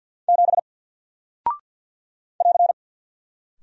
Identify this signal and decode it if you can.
Morse code, heard as a keyed tone, about 50 wpm, 700 Hz. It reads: 7 <AR>
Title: Single Character (Letter:Number:Punctuation:Prosign) 50wpm